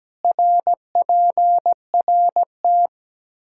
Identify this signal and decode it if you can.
Morse code, heard as a keyed tone, about 17 wpm, 700 Hz. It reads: RPRT
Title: Practice QSO - Exchange 2 - Info - 17wpm